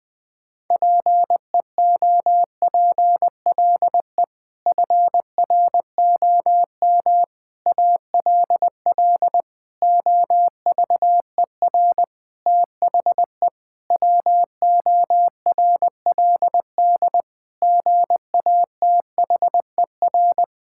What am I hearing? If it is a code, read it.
Morse code, heard as a keyed tone, about 20 wpm, 700 Hz. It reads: PEOPLE FROM ALL OVER THE WORLD GATHER
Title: Sentences with 6-Word Phrases - 20wpm